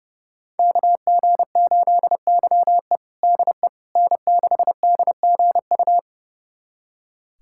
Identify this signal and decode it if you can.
Morse code, heard as a keyed tone, about 30 wpm, 700 Hz. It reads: KG8YE DE N6DGU
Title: USA Callsign DE Callsign - 5 Characters - 30wpm